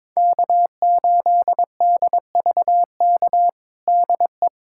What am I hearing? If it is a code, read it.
Morse code, heard as a keyed tone, about 21 wpm, 700 Hz. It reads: K8DVK DE
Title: USA Callsign DE Callsign 22wpm